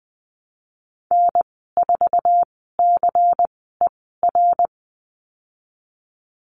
Morse code, heard as a keyed tone, about 20 wpm, spaced as for 14 wpm, 700 Hz. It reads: N4CER
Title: 128 Lesson 21g - Number 6 - Call Signs 20wpm